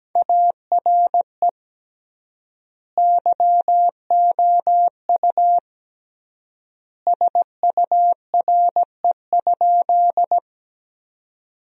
Morse code, heard as a keyed tone, about 17 wpm, 700 Hz. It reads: ARE YOU SURE?
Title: 3x Word-Spacing - Sentences from Top 300 Words - 17wpm